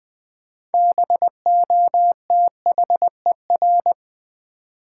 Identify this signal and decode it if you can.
Morse code, heard as a keyed tone, about 21 wpm, 700 Hz. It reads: BOTHER
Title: Six Letter Words - 20wpm